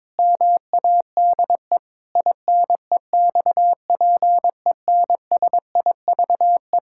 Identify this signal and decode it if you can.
Morse code, heard as a keyed tone, about 22 wpm, 700 Hz. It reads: MADE INEXPENSIVE
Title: Pangram Challenge v2 - 22wpm